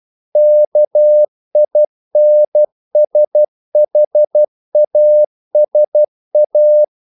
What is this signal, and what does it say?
Morse code, heard as a keyed tone, about 12 wpm, 600 Hz. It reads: KINSHASA
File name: CW / Practice / NationalCapitals12